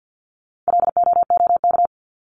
Morse code, heard as a keyed tone, about 50 wpm, 700 Hz. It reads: LOOK
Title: ING Suffix - Top 100 Words with Base Words 50wpm